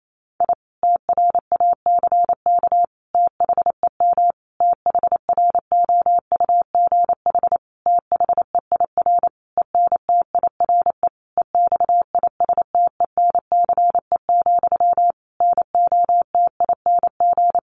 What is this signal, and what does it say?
Morse code, heard as a keyed tone, about 28 wpm, 700 Hz. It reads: I TRACK THEM THROUGH THEIR ENTIRE EXISTENCE, NOTING
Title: A Sound Of Thunder 28wpm